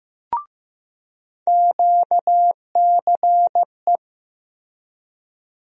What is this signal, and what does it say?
Morse code, heard as a keyed tone, about 15 wpm, 700 Hz. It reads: QCE
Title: USA Call Sign Postfixes - 3 Letters 15wpm